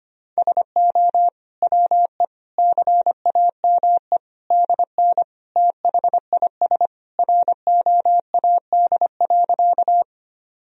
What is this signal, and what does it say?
Morse code, heard as a keyed tone, about 25 wpm, 700 Hz. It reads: SO WE CAME DN THIS ROAD.
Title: Sentences from Top 400 Words and Abbreviations 25wpm